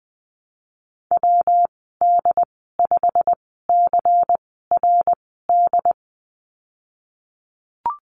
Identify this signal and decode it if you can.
Morse code, heard as a keyed tone, about 20 wpm, spaced as for 15 wpm, 700 Hz. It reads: WD5CRD
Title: 167 Lesson 27g - Number 7 - Call Signs 20wpm